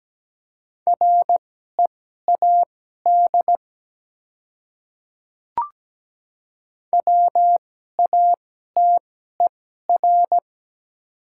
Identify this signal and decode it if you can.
Morse code, heard as a keyed tone, about 17 wpm, spaced as for 12 wpm, 700 Hz. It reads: READ WATER
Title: 2x Character Spacing - Top 300 Words - 17wpm